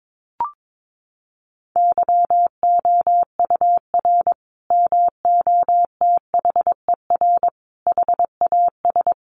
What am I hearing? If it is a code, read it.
Morse code, heard as a keyed tone, about 22 wpm, 700 Hz. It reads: YOUR MOTHER HAS
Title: Sentences from Top 1200 Words 22wpm